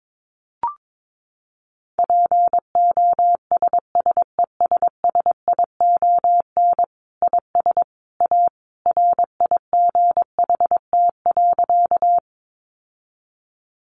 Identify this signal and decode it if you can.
Morse code, heard as a keyed tone, about 22 wpm, 700 Hz. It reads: POSSESSION IS A RIGHT.
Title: Sentences from Top 1400 Words 22wpm